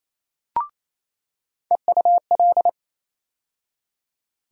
Morse code, heard as a keyed tone, about 28 wpm, 700 Hz. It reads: EUL
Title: USA Call Sign Postfixes - 3 Letters 28wpm